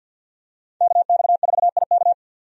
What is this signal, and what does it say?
Morse code, heard as a keyed tone, about 51 wpm, 700 Hz. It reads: KX4IK
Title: USA Call Signs - 5 Characters - 50wpm